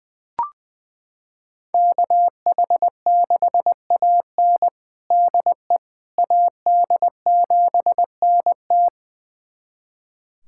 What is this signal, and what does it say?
Morse code, heard as a keyed tone, about 20 wpm, 700 Hz. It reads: KH6AN DE AD7NT
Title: USA Callsign DE Callsign 20wpm